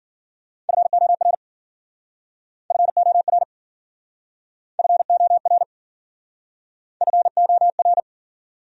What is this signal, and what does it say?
Morse code, heard as a keyed tone, about 52 wpm, 700 Hz. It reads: FOR FOR FOR FOR